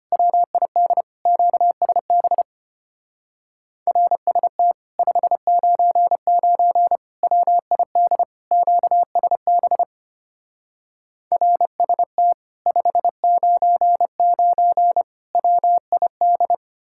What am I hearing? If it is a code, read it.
Morse code, heard as a keyed tone, about 32 wpm, 700 Hz. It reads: WID QSB RST 599 WID QSB RST 599 WID